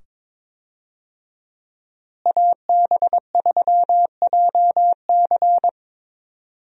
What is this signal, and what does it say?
Morse code, heard as a keyed tone, about 22 wpm, 700 Hz. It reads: AB3JC
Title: USA Call Signs - 5 Characters - Repeated 2x - 22wpm